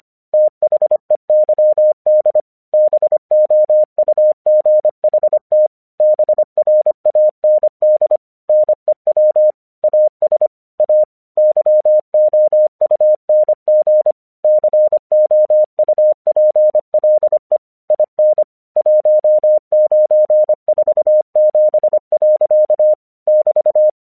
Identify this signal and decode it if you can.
Morse code, heard as a keyed tone, about 25 wpm, 610 Hz. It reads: THEYD BOUGHT BRAND NEW AS A YOUNG COUPLE IN 1947. <BT>
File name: 1 English 25wpm / AQC10